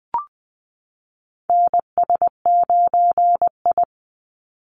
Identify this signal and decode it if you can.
Morse code, heard as a keyed tone, about 20 wpm, 700 Hz. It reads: NS9I
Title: Call Signs - 20wpm